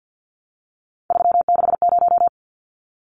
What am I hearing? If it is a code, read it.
Morse code, heard as a keyed tone, about 50 wpm, 700 Hz. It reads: 360
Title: Numbers - 3-Digit 50wpm